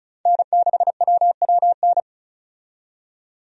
Morse code, heard as a keyed tone, about 34 wpm, 700 Hz. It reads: N6WWN
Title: USA Call Signs - 5 Characters - 35wpm